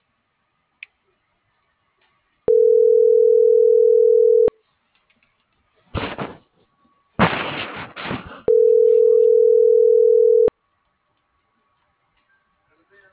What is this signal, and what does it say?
Ringback tone (North America), 440 + 480 Hz.